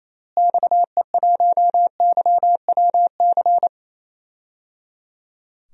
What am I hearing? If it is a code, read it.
Morse code, heard as a keyed tone, about 28 wpm, 700 Hz. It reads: XE1YWC
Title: DXCC-100 Call Signs - 6 Characters - Repeated 3x - 28wpm